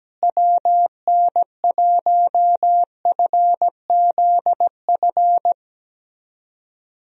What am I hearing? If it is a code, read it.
Morse code, heard as a keyed tone, about 17 wpm, 700 Hz. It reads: WN1FZF
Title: USA Call Signs - 6 Characters - Repeated 2x - 17wpm